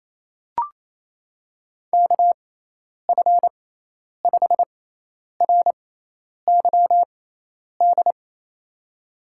Morse code, heard as a keyed tone, about 28 wpm, spaced as for 10 wpm, 700 Hz. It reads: KF5RYD
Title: USA Call Signs - 6x Character Spacing - 28wpm